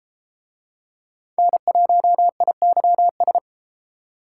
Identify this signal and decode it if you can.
Morse code, heard as a keyed tone, about 31 wpm, 700 Hz. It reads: N1IYS